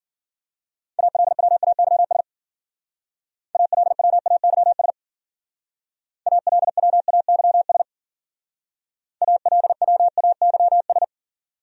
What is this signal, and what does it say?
Morse code, heard as a keyed tone, about 48 wpm, 700 Hz. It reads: ALWAYS ALWAYS ALWAYS ALWAYS
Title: Speed-Racing Top 500 Words 40wpm